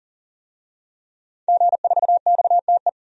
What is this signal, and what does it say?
Morse code, heard as a keyed tone, about 40 wpm, 700 Hz. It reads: G4XTE
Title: DXCC-100 Call Signs - 5 Characters - 40wpm